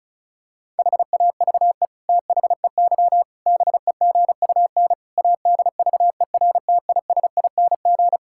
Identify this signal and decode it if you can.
Morse code, heard as a keyed tone, about 34 wpm, 700 Hz. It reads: HAVE THEY BEGUN ADVERTISING
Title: Sentences from Top 1000 Words - Review Part 6 - 35wpm